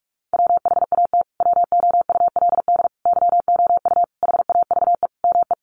Morse code, heard as a keyed tone, about 44 wpm, 700 Hz. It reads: WHAT WOULD YOU HAVE ME
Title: Sentences from Top 200 Words 45wpm